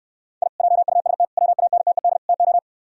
Morse code, heard as a keyed tone, about 68 wpm, 700 Hz. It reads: I JUST WANTED TO